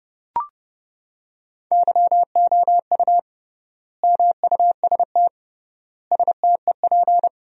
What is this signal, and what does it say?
Morse code, heard as a keyed tone, about 31 wpm, 700 Hz. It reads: YOU MUST STEP
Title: 3x Word-Spacing - Sentences from Top 400 Words - 30wpm